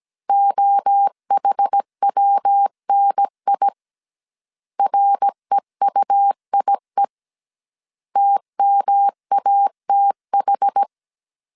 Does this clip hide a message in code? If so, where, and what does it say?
Yes: Morse code from 0.3 to 10.8 s, heard as a keyed tone, about 17 wpm, 800 Hz. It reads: OHWNI REUIE TMATH